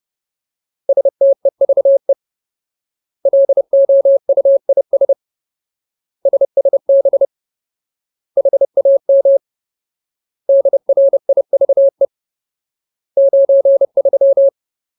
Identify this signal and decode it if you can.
Morse code, heard as a keyed tone, about 30 wpm, 550 Hz. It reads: STEVE LOUIS SSB HAM DRIVE 93
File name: Ultimate Word List / 30WPM Ultimate